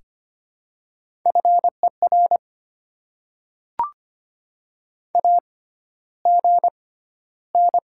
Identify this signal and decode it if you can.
Morse code, heard as a keyed tone, about 25 wpm, spaced as for 13 wpm, 690 Hz. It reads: FER AGN
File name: Top 20 QSO Elements - 6x Character Spacing - 25wpm